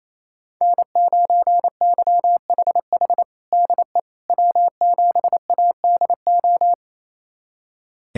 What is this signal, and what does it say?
Morse code, heard as a keyed tone, about 28 wpm, 700 Hz. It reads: N9YHH DE W7ADO
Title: USA Callsign DE Callsign - 5 Characters - 28wpm